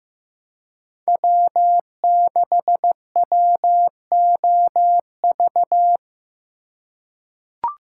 Morse code, heard as a keyed tone, about 15 wpm, 700 Hz. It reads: W6WOV
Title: USA Call Signs - 5 Characters - 2x Character Spacing - 15wpm